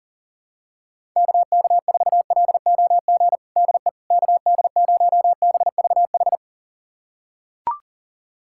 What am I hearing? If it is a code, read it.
Morse code, heard as a keyed tone, about 40 wpm, 700 Hz. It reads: KK4LOG DE KD0BVH
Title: USA Callsign DE Callsign - 6 Characters - 40wpm